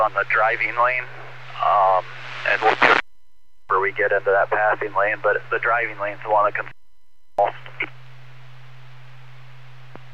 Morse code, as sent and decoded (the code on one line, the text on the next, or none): none